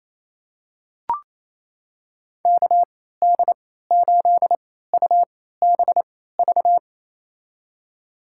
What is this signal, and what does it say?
Morse code, heard as a keyed tone, about 28 wpm, spaced as for 16 wpm, 700 Hz. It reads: KD8UBV
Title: USA Call Signs - 6 Characters - 3x Character Spacing - 28wpm